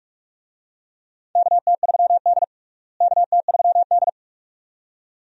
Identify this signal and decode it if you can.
Morse code, heard as a keyed tone, about 46 wpm, 700 Hz. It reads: KT3D KT3D
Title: USA Call Signs - 4 Characters - Repeated 2x - 45wpm